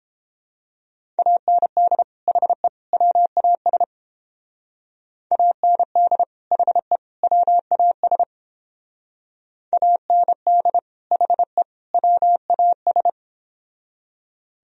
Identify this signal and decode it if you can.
Morse code, heard as a keyed tone, about 31 wpm, 700 Hz. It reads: AND HE WAS AND HE WAS AND HE WAS